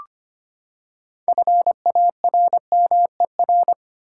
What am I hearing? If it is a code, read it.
Morse code, heard as a keyed tone, about 25 wpm, 700 Hz. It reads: FARMER
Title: Six Letter Words - 25wpm